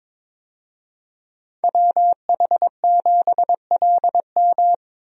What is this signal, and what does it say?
Morse code, heard as a keyed tone, about 22 wpm, 700 Hz. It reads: WH7LM
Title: USA Call Signs - 5 Characters - Repeated 3x - 22wpm